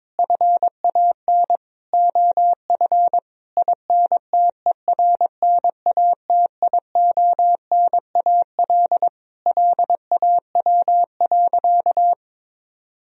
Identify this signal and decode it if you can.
Morse code, heard as a keyed tone, about 22 wpm, 700 Hz. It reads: FAN OF INTERNATIONAL LAW.
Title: Sentences from Top 1000 Words - Review Part 4 - 22wpm